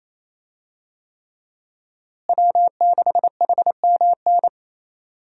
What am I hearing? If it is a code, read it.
Morse code, heard as a keyed tone, about 28 wpm, 700 Hz. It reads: W6HMN
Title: USA Call Signs - 5 Characters - Repeated 3x - 28wpm